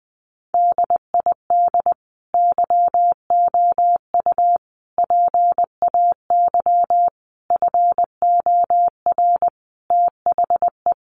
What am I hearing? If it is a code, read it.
Morse code, heard as a keyed tone, about 19 wpm, 700 Hz. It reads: DID YOU PAY FOR THE
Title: Sentences from Top 1500 Words 20wpm